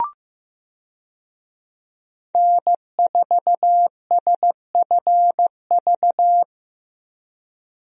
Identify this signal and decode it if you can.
Morse code, heard as a keyed tone, about 15 wpm, 700 Hz. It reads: N4SFV
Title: USA Call Signs - 5 Characters - Repeated 2x - 15wpm